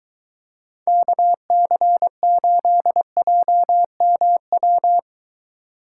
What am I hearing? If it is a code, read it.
Morse code, heard as a keyed tone, about 22 wpm, 700 Hz. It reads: KC8JMW